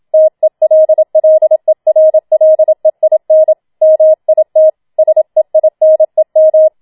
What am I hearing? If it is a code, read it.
Morse code, heard as a keyed tone, about 27 wpm, 630 Hz. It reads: TELLERLEIN MIT SEINEM
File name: Schneewittchen Teil2 27wpm